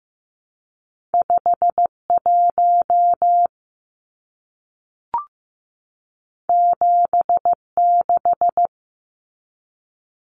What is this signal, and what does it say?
Morse code, heard as a keyed tone, about 15 wpm, 700 Hz. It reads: 51 76
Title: Numbers - 2-Digit 15wpm